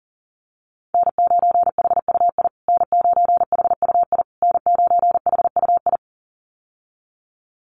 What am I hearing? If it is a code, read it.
Morse code, heard as a keyed tone, about 40 wpm, 700 Hz. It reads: N9HUI N9HUI N9HUI
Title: USA Call Signs Repeated 3x 40wpm